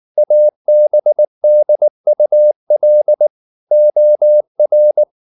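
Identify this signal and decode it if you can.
Morse code, heard as a keyed tone, about 19 wpm, 600 Hz. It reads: ABDUL OR
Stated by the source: Morse code at 19 wpm from CW med Gnister, reading Ken Follett's novel Never in English.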